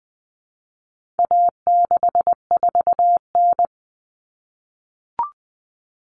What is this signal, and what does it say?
Morse code, heard as a keyed tone, about 20 wpm, 700 Hz. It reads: A64N
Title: Common Cut Numbers - 4-Digit 20wpm